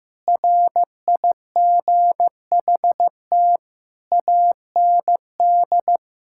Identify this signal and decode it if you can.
Morse code, heard as a keyed tone, about 15 wpm, 700 Hz. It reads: RIGHT AND
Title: Sentences from Top 1600 Words 15wpm